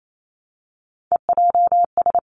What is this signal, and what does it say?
Morse code, heard as a keyed tone, about 28 wpm, 700 Hz. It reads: EJS
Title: Three Challenging Letter-Number Combinations 28wpm